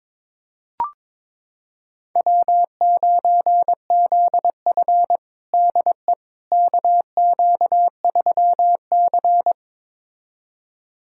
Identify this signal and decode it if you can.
Morse code, heard as a keyed tone, about 21 wpm, 700 Hz. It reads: W9ZF DE KQ3C
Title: USA Callsign DE Callsign - 4 Characters - 22wpm